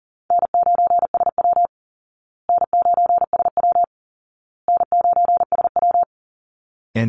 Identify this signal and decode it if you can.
Morse code, heard as a keyed tone, about 40 wpm, 700 Hz. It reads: N9SW N9SW N9SW
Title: Easy Call Signs Repeated 3x - 40wpm